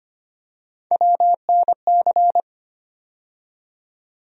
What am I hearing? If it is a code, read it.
Morse code, heard as a keyed tone, about 25 wpm, 700 Hz. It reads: WNC
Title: USA Call Sign Postfixes - 3 Letters 25wpm